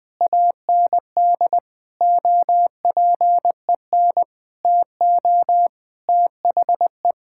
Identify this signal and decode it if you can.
Morse code, heard as a keyed tone, about 20 wpm, 700 Hz. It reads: AND OPEN TO THE
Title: Sentences with 7-Word Phrases - 20wpm